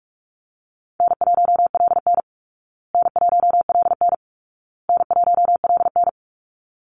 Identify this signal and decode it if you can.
Morse code, heard as a keyed tone, about 44 wpm, 700 Hz. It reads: N1LN N1LN N1LN
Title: Call Signs Repeated 2x - 45wpm